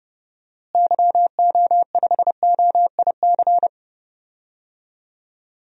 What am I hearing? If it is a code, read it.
Morse code, heard as a keyed tone, about 30 wpm, 700 Hz. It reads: YO5OIC
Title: DXCC-100 Call Signs - 6 Characters - Repeated 2x - 30wpm